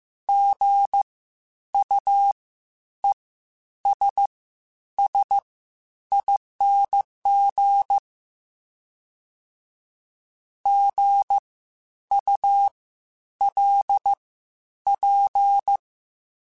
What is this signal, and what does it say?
Morse code, heard as a keyed tone, about 15 wpm, spaced as for 9 wpm, 800 Hz. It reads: GUESSING GULP
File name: Spaced Alphas / Spaced Alphas mp3s / Gs sp @13